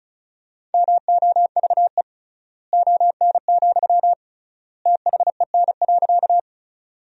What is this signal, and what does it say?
Morse code, heard as a keyed tone, about 36 wpm, 700 Hz. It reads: MOVE ON, THEN.
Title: 3x Word-Spacing - Sentences from Top 200 Words - 35wpm